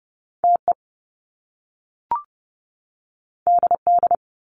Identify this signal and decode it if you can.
Morse code, heard as a keyed tone, about 30 wpm, 700 Hz. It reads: TE DD
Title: Top Two Letter Combinations in Words 30wpm